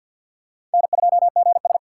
Morse code, heard as a keyed tone, about 52 wpm, 700 Hz. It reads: N2GS
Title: USA Call Signs - 4 Characters - 3x Character Spacing - 50wpm